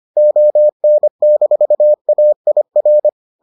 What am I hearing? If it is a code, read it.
Morse code, heard as a keyed tone, about 25 wpm, 600 Hz. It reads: ON-AIR